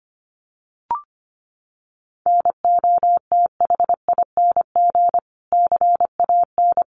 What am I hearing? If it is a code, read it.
Morse code, heard as a keyed tone, about 25 wpm, 700 Hz. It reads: NOTHING CAN
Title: Sentences from Top 1400 Words 25wpm